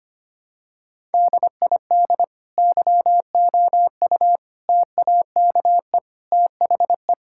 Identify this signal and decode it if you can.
Morse code, heard as a keyed tone, about 25 wpm, 700 Hz. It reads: DID YOU TAKE THE
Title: Sentences from Top 1000 Words - Review Part 6 - 25wpm